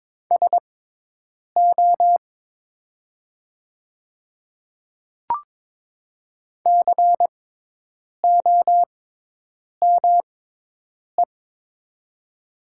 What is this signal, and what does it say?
Morse code, heard as a keyed tone, about 22 wpm, spaced as for 8 wpm, 700 Hz. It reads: SO COME
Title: 6x Character Spacing - Top 500 Words - 22wpm